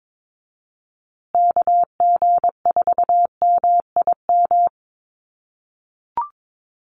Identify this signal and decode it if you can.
Morse code, heard as a keyed tone, about 22 wpm, 700 Hz. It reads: KG4MIM
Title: USA Call Signs Repeated 3x 22wpm